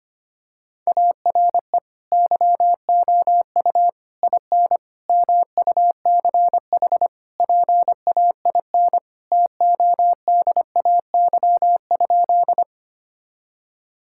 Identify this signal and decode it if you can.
Morse code, heard as a keyed tone, about 25 wpm, 700 Hz. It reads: ARE YOU IN MUCH PAIN TODAY?
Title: Sentences from Top 1600 Words 25wpm